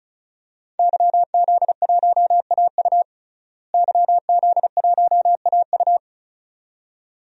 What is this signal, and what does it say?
Morse code, heard as a keyed tone, about 36 wpm, 700 Hz. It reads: YZ1AU YZ1AU
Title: DXCC-100 Call Signs - 5 Characters - Repeated 2x - 35wpm